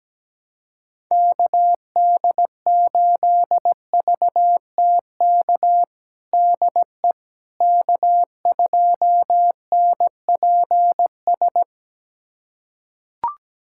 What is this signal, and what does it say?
Morse code, heard as a keyed tone, about 17 wpm, 700 Hz. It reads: KD8VTK DE K2NPS
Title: USA Callsign DE Callsign 17wpm